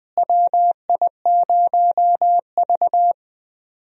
Morse code, heard as a keyed tone, about 20 wpm, 700 Hz. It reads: WI0V
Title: USA Call Signs - 4 Characters - Repeated 2x - 20wpm